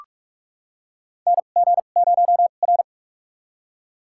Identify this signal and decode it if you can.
Morse code, heard as a keyed tone, about 46 wpm, spaced as for 32 wpm, 700 Hz. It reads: NG0R
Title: USA Call Signs - 4 Characters - 2x Character Spacing - 45wpm